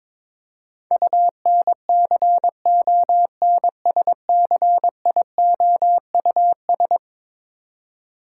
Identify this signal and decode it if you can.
Morse code, heard as a keyed tone, about 21 wpm, 700 Hz. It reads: UNCONSCIOUS
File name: practice / 22wpm / Eleven Letter Words - 22wpm